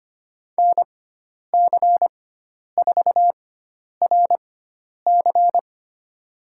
Morse code, heard as a keyed tone, about 25 wpm, spaced as for 10 wpm, 700 Hz. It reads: NC4RC
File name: course / 25:10wpm / 095 Lesson 16g - Letter c - Call Signs 25wpm